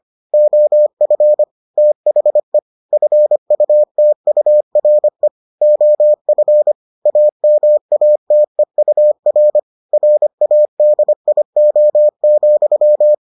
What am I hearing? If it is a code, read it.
Morse code, heard as a keyed tone, about 25 wpm, 600 Hz. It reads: OF THE FUTURE OF AMATEUR RADIO,